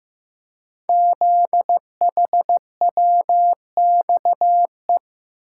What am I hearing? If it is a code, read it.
Morse code, heard as a keyed tone, about 15 wpm, 700 Hz. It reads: ZHWXE
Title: Code Groups - 5 Letters - 15wpm